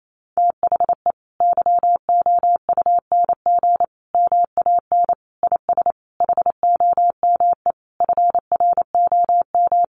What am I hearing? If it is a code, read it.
Morse code, heard as a keyed tone, about 28 wpm, 700 Hz. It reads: THE YOUNG MAN IS HOME FROM
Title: Sentences from Top 1300 Words 28wpm